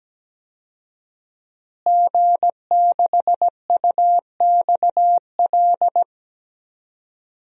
Morse code, heard as a keyed tone, about 17 wpm, 700 Hz. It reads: G6UXL
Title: DXCC-100 Call Signs - 5 Characters - Repeated 3x - 17wpm